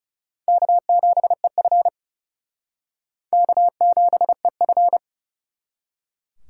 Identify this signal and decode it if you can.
Morse code, heard as a keyed tone, about 34 wpm, 700 Hz. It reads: K7EF K7EF
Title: Speed Racing - USA 1x2 Call Signs 30wpm